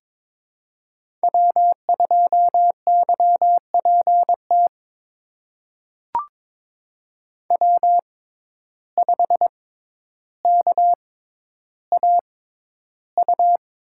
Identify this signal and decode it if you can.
Morse code, heard as a keyed tone, about 22 wpm, spaced as for 11 wpm, 700 Hz. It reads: W2YPT W5KAU
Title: USA Call Signs - 5 Characters - 6x Character Spacing - 22wpm